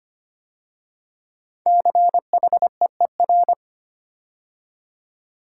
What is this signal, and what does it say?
Morse code, heard as a keyed tone, about 25 wpm, 700 Hz. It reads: CHEER
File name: 5 Letter Words - First 3 Letters Given - 25wpm